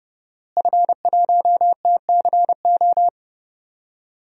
Morse code, heard as a keyed tone, about 30 wpm, 700 Hz. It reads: F1TCO
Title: DXCC-100 Call Signs - 5 Characters - Repeated 3x - 30wpm